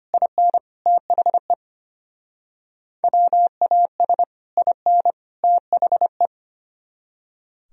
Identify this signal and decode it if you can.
Morse code, heard as a keyed tone, about 33 wpm, 700 Hz. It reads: IN THE WAS IN THE